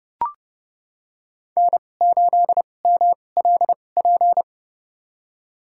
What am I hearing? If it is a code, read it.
Morse code, heard as a keyed tone, about 30 wpm, spaced as for 22 wpm, 700 Hz. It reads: N8MLP
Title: USA Call Signs - 5 Characters - 2x Character Spacing - 30wpm